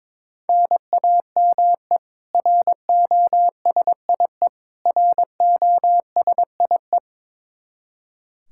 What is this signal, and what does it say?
Morse code, heard as a keyed tone, about 22 wpm, 700 Hz. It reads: NAME ROSIE ROSIE
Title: K1USN SST QSO Practice 22wpm